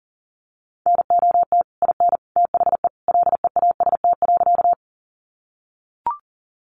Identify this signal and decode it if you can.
Morse code, heard as a keyed tone, about 40 wpm, 700 Hz. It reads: NOT IN THE LEAST.
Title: Sentences from Top 700 Words 40wpm